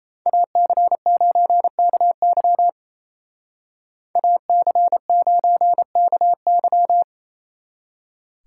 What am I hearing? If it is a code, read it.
Morse code, heard as a keyed tone, about 36 wpm, 700 Hz. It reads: AC9KY AC9KY